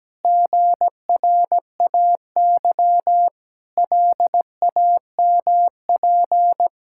Morse code, heard as a keyed tone, about 17 wpm, 700 Hz. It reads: GRAY LAMP
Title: Pangram Challenge v2 - 17wpm